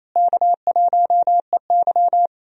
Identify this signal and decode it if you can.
Morse code, heard as a keyed tone, about 28 wpm, 700 Hz. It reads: K1EY
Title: USA Call Signs - 4 Characters - Repeated 2x - 28wpm